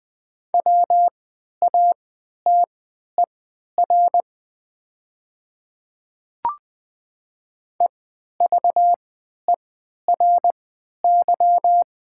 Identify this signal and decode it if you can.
Morse code, heard as a keyed tone, about 20 wpm, spaced as for 11 wpm, 700 Hz. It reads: WATER EVERY
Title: 3x Character Spacing - Top 500 Words - 20wpm